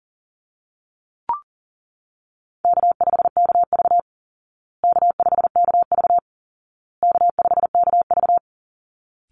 Morse code, heard as a keyed tone, about 40 wpm, 700 Hz. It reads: K5KV K5KV K5KV
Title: Easy Call Signs Repeated 3x - 40wpm